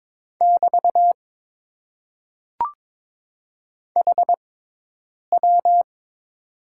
Morse code, heard as a keyed tone, about 22 wpm, spaced as for 8 wpm, 700 Hz. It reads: <BT> HW
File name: Top 90 QSO Elements - 6x Character Spacing - 22wpm